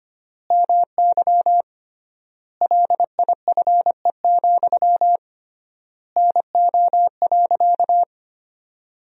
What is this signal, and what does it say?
Morse code, heard as a keyed tone, about 25 wpm, 700 Hz. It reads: MY LIFE, NO.
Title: 3x Word-Spacing - Sentences from Top 300 Words - 25wpm